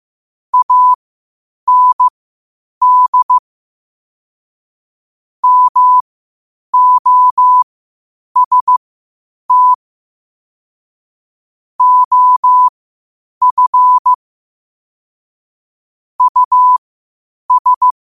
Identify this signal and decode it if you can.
Morse code, heard as a keyed tone, about 15 wpm, spaced as for 8 wpm, 1000 Hz. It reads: AND MOST OF US
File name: 8wpm